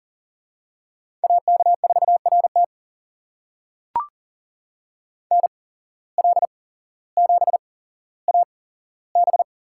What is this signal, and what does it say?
Morse code, heard as a keyed tone, about 37 wpm, spaced as for 17 wpm, 700 Hz. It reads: AK4RT NL7AB
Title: USA Call Signs - 5 Characters - 8x Character Spacing - 40wpm